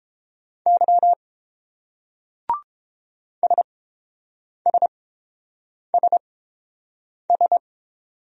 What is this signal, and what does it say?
Morse code, heard as a keyed tone, about 29 wpm, spaced as for 7 wpm, 700 Hz. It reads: Y SSSS